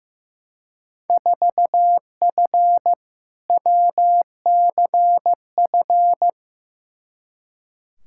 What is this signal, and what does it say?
Morse code, heard as a keyed tone, about 15 wpm, 700 Hz. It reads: 4F WCF
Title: Field Day Exchanges - 15wpm